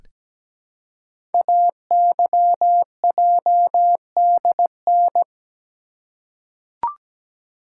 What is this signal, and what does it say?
Morse code, heard as a keyed tone, about 17 wpm, 700 Hz. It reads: AYJDN
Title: Code Groups - 5 Letters - 17wpm